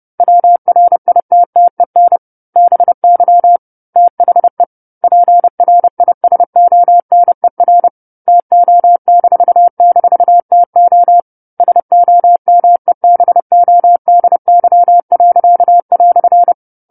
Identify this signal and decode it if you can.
Morse code, heard as a keyed tone, about 30 wpm, 700 Hz. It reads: WRITTEN BY THE PRISONER TO--TO SOMEBODY."
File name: 10-Chapter-10 30 WPM